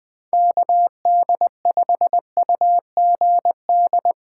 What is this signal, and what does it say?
Morse code, heard as a keyed tone, about 21 wpm, 700 Hz. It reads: KD5UGD
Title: USA Call Signs - 6 Characters - 6x Character Spacing - 20wpm